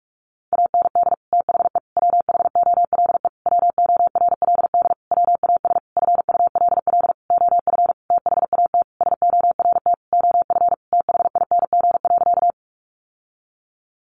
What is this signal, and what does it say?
Morse code, heard as a keyed tone, about 44 wpm, 700 Hz. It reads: AND THE WHOLE WORLD WAS FULL OF THAT SORT OF THING.
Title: Sentences from Top 1300 Words 45wpm